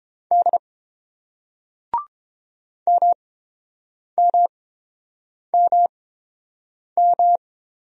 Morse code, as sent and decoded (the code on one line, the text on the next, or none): -.. / -- -- -- --
D MMMM